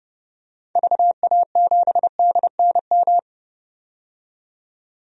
Morse code, heard as a keyed tone, about 32 wpm, 700 Hz. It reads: VA7DNM
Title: DXCC-100 Call Signs - 6 Characters - 30wpm